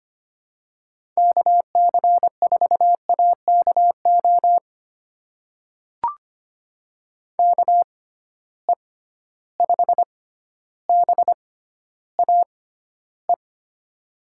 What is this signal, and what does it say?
Morse code, heard as a keyed tone, about 25 wpm, spaced as for 13 wpm, 700 Hz. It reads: KC4AKO KE5BAE
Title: USA Call Signs - 6 Characters - 6x Character Spacing - 25wpm